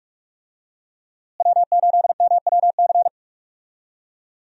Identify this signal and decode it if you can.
Morse code, heard as a keyed tone, about 41 wpm, 700 Hz. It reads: W8MWC
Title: USA Call Signs - 5 Characters - Repeated 2x - 45wpm